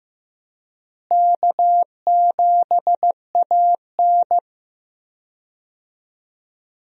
Morse code, heard as a keyed tone, about 15 wpm, 700 Hz. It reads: K7AN
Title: USA Call Signs - 4 Characters - Repeated 2x - 15wpm